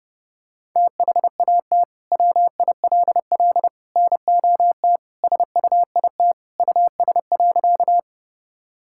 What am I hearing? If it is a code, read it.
Morse code, heard as a keyed tone, about 32 wpm, 700 Hz. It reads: THAT WILL NOT SUIT US.
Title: Sentences from Top 1600 Words 30wpm